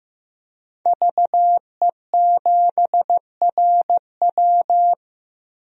Morse code, heard as a keyed tone, about 15 wpm, 700 Hz. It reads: VE7RW
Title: DXCC-100 Call Signs - 5 Characters - Repeated 3x - 15wpm